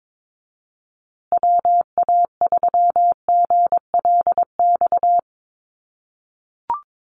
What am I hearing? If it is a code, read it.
Morse code, heard as a keyed tone, about 22 wpm, 700 Hz. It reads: WA3GLX
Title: USA Call Signs Repeated 2x 22wpm